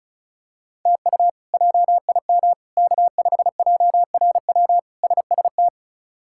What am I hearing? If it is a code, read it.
Morse code, heard as a keyed tone, about 34 wpm, 700 Hz. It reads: TU JIM K5JRW SST
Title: K1USN SST QSO Practice - 35wpm